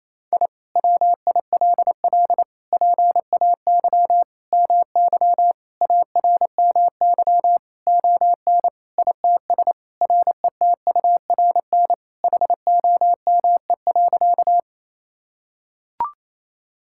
Morse code, heard as a keyed tone, about 28 wpm, 700 Hz. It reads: I WILL PAY MY ARMY ON ITS RETURN HOME.
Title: Sentences from Top 1000 Words - Review Part 5 - 28wpm